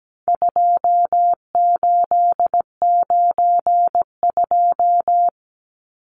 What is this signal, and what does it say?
Morse code, heard as a keyed tone, about 17 wpm, 700 Hz. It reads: 2892
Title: Numbers - 4-Digit 17wpm